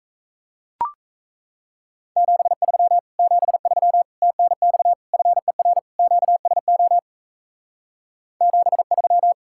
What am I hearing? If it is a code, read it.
Morse code, heard as a keyed tone, about 37 wpm, 700 Hz. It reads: 73 73 TNX FER QSO 73